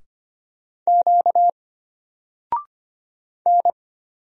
Morse code, heard as a keyed tone, about 25 wpm, 700 Hz. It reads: Q N
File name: Single-Letter-Number-25wpm